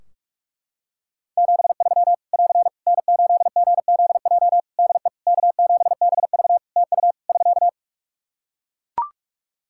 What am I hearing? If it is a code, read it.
Morse code, heard as a keyed tone, about 44 wpm, 700 Hz. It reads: 73 <AR> N8GZJ DE K7BV TU <SK>
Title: Practice QSO - Exchange 4 - Ending - 45wpm